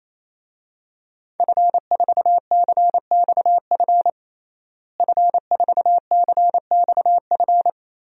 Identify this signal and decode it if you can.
Morse code, heard as a keyed tone, about 28 wpm, 700 Hz. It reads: F4CXF F4CXF
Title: DXCC-100 Call Signs - 5 Characters - Repeated 3x - 28wpm